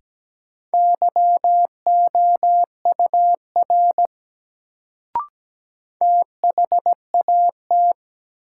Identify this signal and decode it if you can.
Morse code, heard as a keyed tone, about 17 wpm, 700 Hz. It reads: YOUR THAT
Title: Sentences from Top 100 Words - New Word Drill - 17wpm